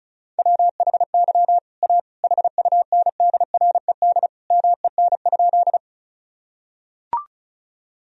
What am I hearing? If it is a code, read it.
Morse code, heard as a keyed tone, about 36 wpm, 700 Hz. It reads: WHY A HUNDRED MEN?
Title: Sentences from Top 1000 Words - Review Part 2 - 35wpm